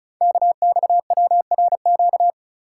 Morse code, heard as a keyed tone, about 36 wpm, 690 Hz. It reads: KXWRQ
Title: Code Groups - 5 Letters - 35wpm